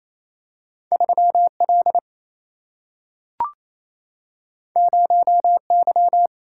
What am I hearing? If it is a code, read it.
Morse code, heard as a keyed tone, about 28 wpm, 700 Hz. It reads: 3 L 0 Y
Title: Single-Letter Single-Number 28wpm